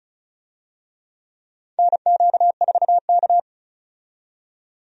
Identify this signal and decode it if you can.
Morse code, heard as a keyed tone, about 34 wpm, 700 Hz. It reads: NQ4K
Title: USA Call Signs - 4 Characters - 35wpm